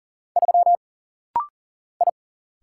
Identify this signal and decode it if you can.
Morse code, heard as a keyed tone, about 41 wpm, 700 Hz. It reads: 3 I
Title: Single Letter-Number - Rapid-Fire 40wpm